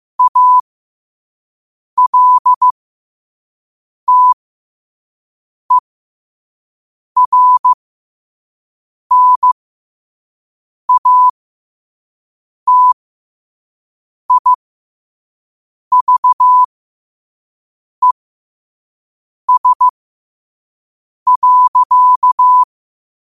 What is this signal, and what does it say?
Morse code, heard as a keyed tone, about 15 wpm, spaced as for 5 wpm, 1000 Hz. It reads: ALTERNATIVES.
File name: 5wpm